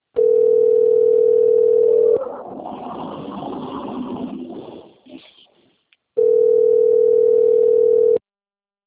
Ringback tone (North America), 440 + 480 Hz.
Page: Duque de Caxias - RJ